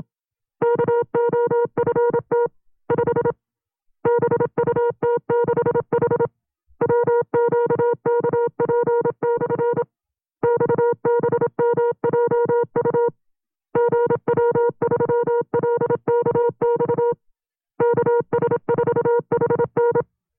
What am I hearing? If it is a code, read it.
Morse code, heard as a keyed tone, about 27 wpm, 470 Hz. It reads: KOFT 5 BUT6H WQKP/ XBMJU GW3LKX KS4HN
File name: FreeFiles / CW-practice / 25WPM-side2